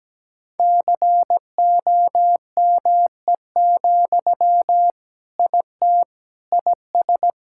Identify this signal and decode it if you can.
Morse code, heard as a keyed tone, about 17 wpm, 700 Hz. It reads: COME, IT IS
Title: Sentences from Top 1600 Words 17wpm